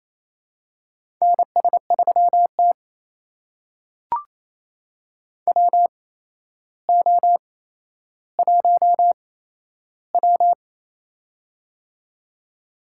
Morse code, heard as a keyed tone, about 28 wpm, spaced as for 12 wpm, 700 Hz. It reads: NS3T WO1W
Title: USA Call Signs - 4 Characters - 8x Character Spacing - 28wpm